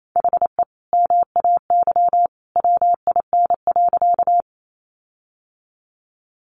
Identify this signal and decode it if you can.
Morse code, heard as a keyed tone, about 28 wpm, 700 Hz. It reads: HE MAY WIN.
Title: Sentences from Top 1500 Words 28wpm